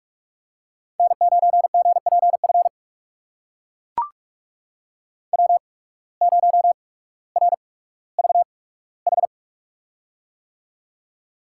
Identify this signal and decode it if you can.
Morse code, heard as a keyed tone, about 42 wpm, spaced as for 19 wpm, 700 Hz. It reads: N9GPF W0RVH
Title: USA Call Signs - 5 Characters - 8x Character Spacing - 45wpm